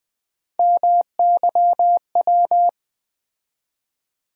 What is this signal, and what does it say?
Morse code, heard as a keyed tone, about 20 wpm, 700 Hz. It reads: MYW
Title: USA Call Sign Postfixes - 3 Letters 20wpm